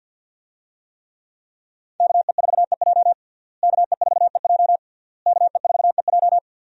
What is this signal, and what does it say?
Morse code, heard as a keyed tone, about 52 wpm, 700 Hz. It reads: KE4EJ KE4EJ KE4EJ
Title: USA Call Signs - 5 Characters - Repeated 3x - 50wpm